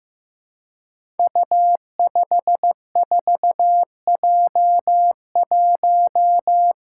Morse code, heard as a keyed tone, about 15 wpm, 700 Hz. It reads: U54J1
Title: Five Challenging Letter-Number Combinations 15wpm